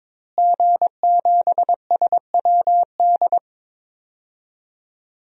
Morse code, heard as a keyed tone, about 22 wpm, 700 Hz. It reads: G7SWD
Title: DXCC-100 Call Signs - 5 Characters - Repeated 2x - 22wpm